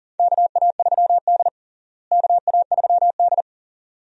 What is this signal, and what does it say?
Morse code, heard as a keyed tone, about 40 wpm, 700 Hz. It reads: KA3D KA3D
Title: USA Call Signs - 4 Characters - Repeated 2x - 40wpm